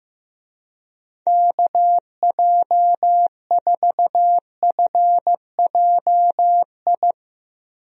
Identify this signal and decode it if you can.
Morse code, heard as a keyed tone, about 15 wpm, 700 Hz. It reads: KJ4FJI
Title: USA Call Signs - 6 Characters - Repeated 2x - 15wpm